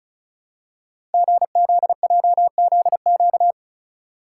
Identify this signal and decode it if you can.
Morse code, heard as a keyed tone, about 36 wpm, 700 Hz. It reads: GZJZQ
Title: Code Groups - 5 Letter:Numbers - 35wpm